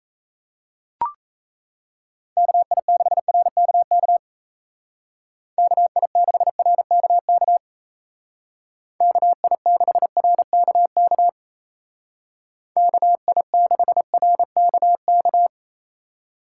Morse code, heard as a keyed tone, about 34 wpm, 700 Hz. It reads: KI6RKK KI6RKK KI6RKK KI6RKK
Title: Speed Racing - USA 2x3 Call Signs 28wpm